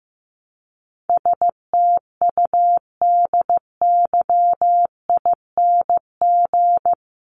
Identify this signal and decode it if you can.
Morse code, heard as a keyed tone, about 15 wpm, 700 Hz. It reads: STUDYING
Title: A Sound Of Thunder 15wpm